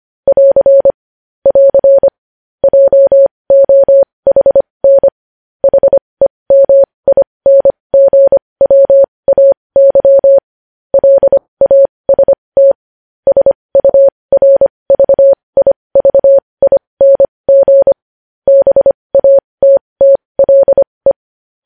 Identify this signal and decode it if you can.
Morse code, heard as a keyed tone, about 25 wpm, spaced as for 20 wpm, 550 Hz. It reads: <AR> <AR> JOHN HEMINGWAY LAST SURVIVING BATTLE
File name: podcast / News headlines-1x-25x20